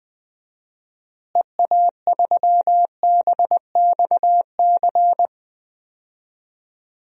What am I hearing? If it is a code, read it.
Morse code, heard as a keyed tone, about 20 wpm, 700 Hz. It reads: EA3BXC
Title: DXCC-100 Call Signs - 6 Characters - Repeated 3x - 20wpm